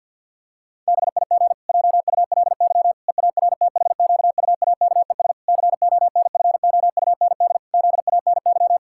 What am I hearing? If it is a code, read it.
Morse code, heard as a keyed tone, about 50 wpm, 700 Hz. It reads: BIG JULY EARTHQUAKES CONFOUND ZANY
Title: Pangram Challenge v2 - 50wpm